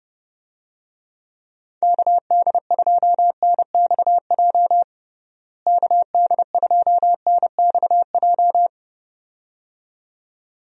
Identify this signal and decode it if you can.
Morse code, heard as a keyed tone, about 32 wpm, 700 Hz. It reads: KD2NXJ KD2NXJ
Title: USA Call Signs - 6 Characters - Repeated 2x - 30wpm